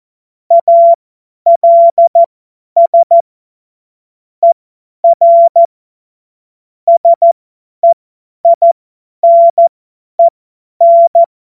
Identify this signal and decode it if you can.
Morse code, heard as a keyed tone, about 14 wpm, spaced as for 10 wpm, 680 Hz. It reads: ALS ER SEINEN
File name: Morsekurs / Witze / Witz 01. Februar 2022 50 F70